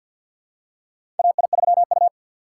Morse code, heard as a keyed tone, about 50 wpm, 700 Hz. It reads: AI3U
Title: USA Call Signs - 4 Characters - 6x Character Spacing - 50wpm